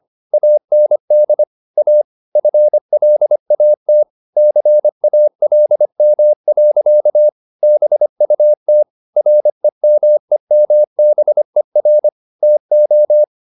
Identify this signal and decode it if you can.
Morse code, heard as a keyed tone, about 25 wpm, 600 Hz. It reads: AND A FLAT CALM. BUT REMEMBER TO